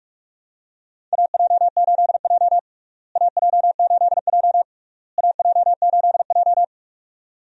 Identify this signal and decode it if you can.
Morse code, heard as a keyed tone, about 42 wpm, 700 Hz. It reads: AJ8J AJ8J AJ8J
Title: USA Call Signs - 4 Characters - Repeated 3x - 45wpm